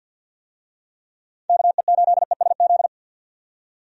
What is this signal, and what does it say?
Morse code, heard as a keyed tone, about 51 wpm, 700 Hz. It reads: KE8ESZ
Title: USA Call Signs - 6 Characters - 50wpm